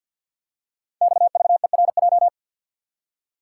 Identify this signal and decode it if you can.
Morse code, heard as a keyed tone, about 52 wpm, 690 Hz. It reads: XVERJ
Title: Code Groups - 5 Letters - 50wpm